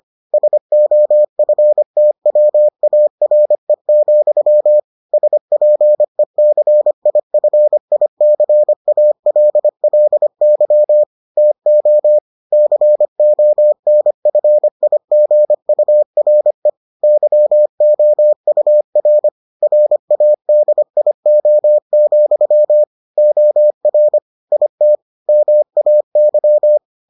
Morse code, heard as a keyed tone, about 25 wpm, 600 Hz. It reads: SOFTWARE, SPECIFICALLY TO CONFIGURE YOUR RADIO, OR IT MAY